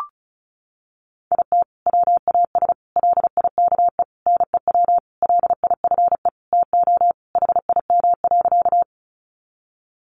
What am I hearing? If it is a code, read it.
Morse code, heard as a keyed tone, about 34 wpm, 700 Hz. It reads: IT WAS LIKE NEW LIFE TO HIM.
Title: Sentences from Top 300 Words 35wpm